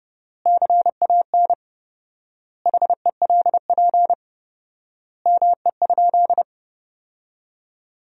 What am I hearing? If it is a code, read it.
Morse code, heard as a keyed tone, about 30 wpm, 700 Hz. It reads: CAN HELP ME?
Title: 4x Word-Spacing - Sentences from Top 200 Words - 30wpm